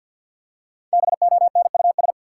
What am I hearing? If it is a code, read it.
Morse code, heard as a keyed tone, about 47 wpm, 700 Hz. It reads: BONUS
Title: Five Letter Words - 50wpm